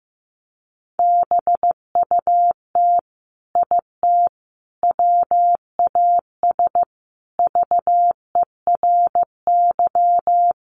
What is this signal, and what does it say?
Morse code, heard as a keyed tone, about 15 wpm, 700 Hz. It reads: BUT IT WAS VERY
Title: Sentences from Top 300 Words 15wpm